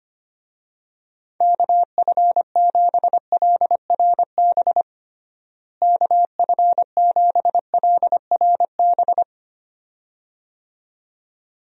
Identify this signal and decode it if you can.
Morse code, heard as a keyed tone, about 25 wpm, 700 Hz. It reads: KF7LRB KF7LRB
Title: USA Call Signs - 6 Characters - Repeated 2x - 25wpm